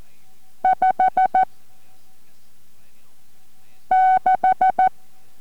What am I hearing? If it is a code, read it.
Morse code, heard as a keyed tone, about 14 wpm, 750 Hz. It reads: 5 6